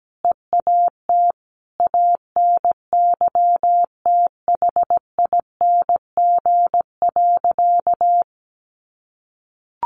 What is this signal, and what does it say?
Morse code, heard as a keyed tone, about 17 wpm, 700 Hz. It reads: EAT ANYTHING.
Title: Sentences from Top 1000 Words 17wpm